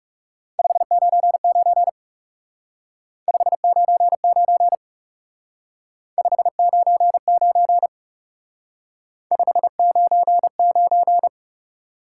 Morse code, heard as a keyed tone, about 38 wpm, 700 Hz. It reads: 599 599 599 599
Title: Speed-Racing Top 90 QSO Elements 30wpm